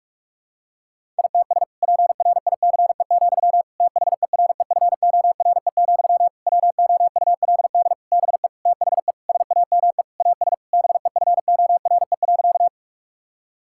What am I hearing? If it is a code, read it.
Morse code, heard as a keyed tone, about 42 wpm, 700 Hz. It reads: ITS PRICE, THEREFORE, WOULD BE THE SAME AS BEFORE.
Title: Sentences from Top 1000 Words - Review Part 5 - 45wpm